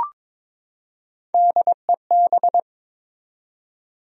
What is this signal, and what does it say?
Morse code, heard as a keyed tone, about 23 wpm, 700 Hz. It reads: DEB
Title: USA Common Names - Max 5 Letters - 22wpm